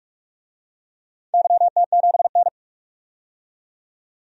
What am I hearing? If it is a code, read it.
Morse code, heard as a keyed tone, about 44 wpm, 700 Hz. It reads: YT7N
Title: DXCC-100 Call Signs - 4 Characters - 45wpm